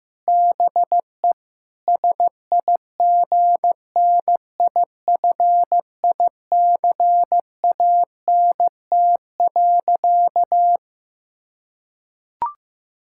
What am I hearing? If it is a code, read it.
Morse code, heard as a keyed tone, about 15 wpm, 700 Hz. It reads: BE SIGNIFICANT.
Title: Sentences from Top 1000 Words - Review Part 6 - 15wpm